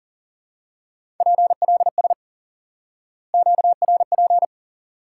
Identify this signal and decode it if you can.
Morse code, heard as a keyed tone, about 40 wpm, 700 Hz. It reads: PLS QRP
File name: IQE Course 3 - Lesson 89 - YAGI - Two Elements - 40wpm